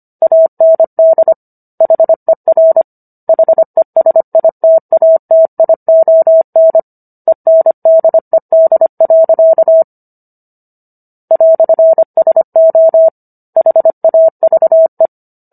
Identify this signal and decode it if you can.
Morse code, heard as a keyed tone, about 25 wpm, 650 Hz. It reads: AND HER HESITATION ENDED. "SO HAVE